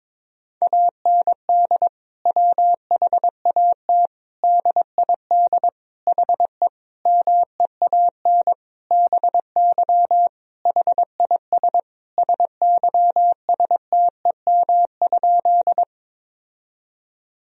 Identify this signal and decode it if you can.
Morse code, heard as a keyed tone, about 22 wpm, 700 Hz. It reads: AND WHAT DID HE MEAN BY HIS SYSTEM?
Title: Sentences from Top 1000 Words - Review Part 3 - 22wpm